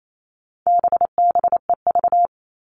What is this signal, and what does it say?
Morse code, heard as a keyed tone, about 28 wpm, 700 Hz. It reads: BBEV
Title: All Cut Numbers - 4-Digit 28wpm